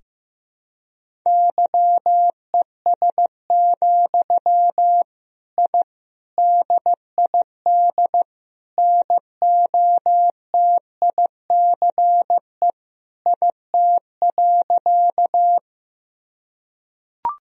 Morse code, heard as a keyed tone, about 16 wpm, 700 Hz. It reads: YES, I DID NOTICE IT.
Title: Sentences from Top 1000 Words - Review Part 2 - 15wpm